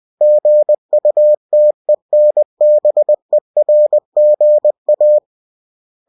Morse code, heard as a keyed tone, about 20 wpm, 600 Hz. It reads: GUTENBERGA
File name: cw-books / winnie-the-pooh / winnie-20wpm-page-0122